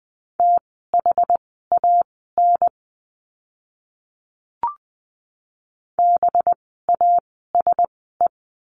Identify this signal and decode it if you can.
Morse code, heard as a keyed tone, about 20 wpm, spaced as for 14 wpm, 700 Hz. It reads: THAN BASE
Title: 185 Lesson 30f - Letter b - Words 20wpm